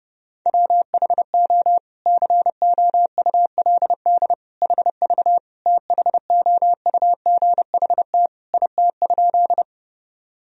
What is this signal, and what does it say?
Morse code, heard as a keyed tone, about 30 wpm, 700 Hz. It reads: WHO COULD HV THOUGHT IT?
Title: Sentences from Top 300 Words and Abbreviations 30wpm